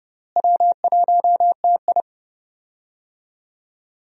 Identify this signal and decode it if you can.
Morse code, heard as a keyed tone, about 30 wpm, 700 Hz. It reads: W1TI
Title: USA Call Signs - 4 Characters - Repeated 2x - 30wpm